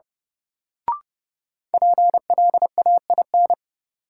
Morse code, heard as a keyed tone, about 30 wpm, 700 Hz. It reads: PLAIN
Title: Sentences from Top 400 Words - New Word Drill - 30wpm